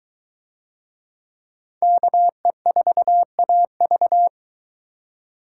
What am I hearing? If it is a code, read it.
Morse code, heard as a keyed tone, about 21 wpm, 700 Hz. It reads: KE4AV